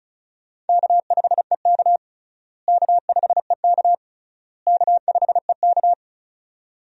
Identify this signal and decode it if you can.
Morse code, heard as a keyed tone, about 36 wpm, 700 Hz. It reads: K5EK K5EK K5EK
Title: USA Call Signs - 4 Characters - Repeated 3x - 35wpm